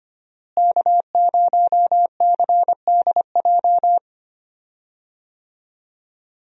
Morse code, heard as a keyed tone, about 25 wpm, 700 Hz. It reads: K0CDJ
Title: USA Call Signs - 5 Characters - Repeated 2x - 25wpm